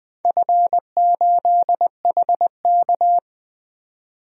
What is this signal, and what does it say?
Morse code, heard as a keyed tone, about 20 wpm, 700 Hz. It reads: F8HK
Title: DXCC-100 Call Signs - 4 Characters - 20wpm